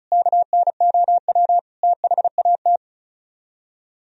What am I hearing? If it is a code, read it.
Morse code, heard as a keyed tone, about 46 wpm, spaced as for 41 wpm, 700 Hz. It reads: KNOW THAT
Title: Speed-Racing - Sets of 4 Words 35wpm